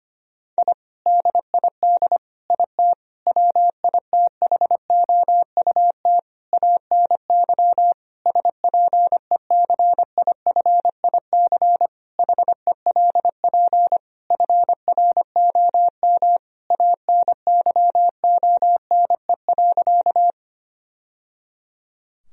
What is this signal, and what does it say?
Morse code, heard as a keyed tone, about 25 wpm, 700 Hz. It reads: I DID IT WITHOUT ANY SPECIFIC HELP FROM ANYONE.
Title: Sentences from Top 1000 Words - Review Part 3 - 25wpm